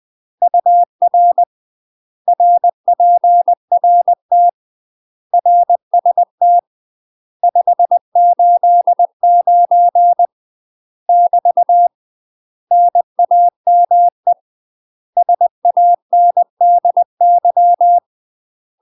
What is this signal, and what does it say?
Morse code, heard as a keyed tone, about 20 wpm, 700 Hz. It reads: UR RPRT RST 589 <BT> NAME SANDY
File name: 20WPM